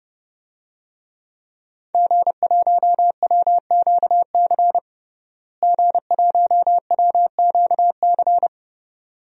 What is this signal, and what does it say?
Morse code, heard as a keyed tone, about 30 wpm, 700 Hz. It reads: G1WQC G1WQC
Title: DXCC-100 Call Signs - 5 Characters - Repeated 2x - 30wpm